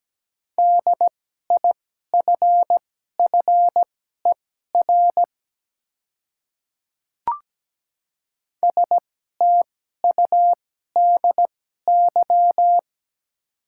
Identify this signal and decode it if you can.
Morse code, heard as a keyed tone, about 17 wpm, spaced as for 12 wpm, 700 Hz. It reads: DIFFER STUDY
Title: 2x Character Spacing - Top 300 Words - 17wpm